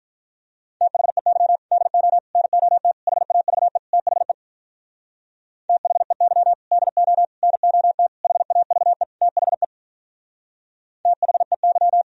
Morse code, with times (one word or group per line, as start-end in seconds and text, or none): 0.8-1.6 THEY
1.7-2.2 DO
2.4-2.9 NOT
3.1-3.8 HAVE
3.9-4.3 THE
5.7-6.5 THEY
6.7-7.2 DO
7.4-8.1 NOT
8.2-9.0 HAVE
9.2-9.6 THE
11.1-12.0 THEY